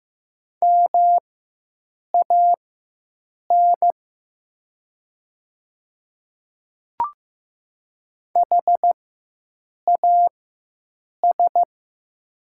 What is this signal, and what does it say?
Morse code, heard as a keyed tone, about 15 wpm, spaced as for 7 wpm, 700 Hz. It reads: MAN HAS
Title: 4x Character Spacing - Top 500 Words - 15wpm